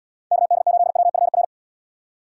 Morse code, heard as a keyed tone, about 77 wpm, 700 Hz. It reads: KD8WFR